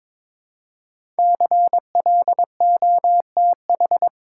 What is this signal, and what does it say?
Morse code, heard as a keyed tone, about 22 wpm, 700 Hz. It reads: CLOTH
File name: 5 Letter Words - First 3 Letters Given - 22wpm